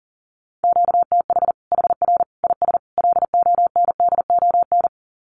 Morse code, heard as a keyed tone, about 40 wpm, 700 Hz. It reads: QTH HR IS LONDON
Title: Common QSO Phrases 40wpm